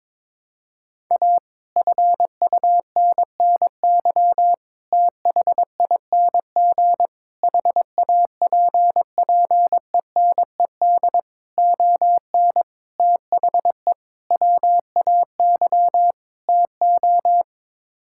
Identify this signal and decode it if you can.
Morse code, heard as a keyed tone, about 22 wpm, 700 Hz. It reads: A FUNNY THING HAPPENED ON THE WAY TO
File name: practice / 22wpm / Sets of 8 Words - 22wpm